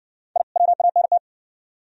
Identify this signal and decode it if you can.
Morse code, heard as a keyed tone, about 60 wpm, 700 Hz. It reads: I WANT